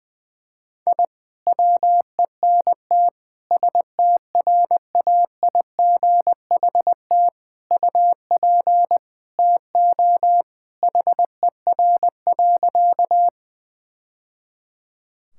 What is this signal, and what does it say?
Morse code, heard as a keyed tone, about 21 wpm, 700 Hz. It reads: I WENT STRAIGHT UP TO HER.
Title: Sentences from Top 1000 Words - Review Part 6 - 20wpm